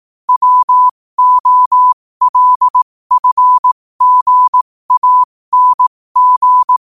Morse code, heard as a keyed tone, about 18 wpm, spaced as for 15 wpm, 1000 Hz. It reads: WOLFGANG
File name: qotdfiles / 15wpm